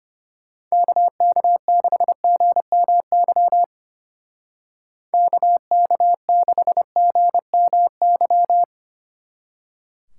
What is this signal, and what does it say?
Morse code, heard as a keyed tone, about 30 wpm, 700 Hz. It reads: KK6GMY KK6GMY
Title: Speed Racing - USA 2x3 Call Signs 25wpm